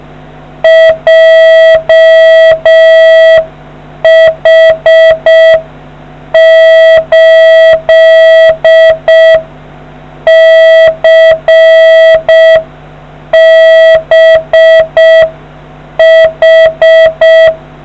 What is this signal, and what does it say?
Morse code, heard as a keyed tone, about 6 wpm, 1950 Hz. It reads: JH8CBH